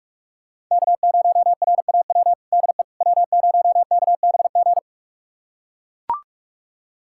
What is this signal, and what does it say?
Morse code, heard as a keyed tone, about 44 wpm, 700 Hz. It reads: K0RAW DE W0KBG
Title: USA Callsign DE Callsign - 5 Characters - 45wpm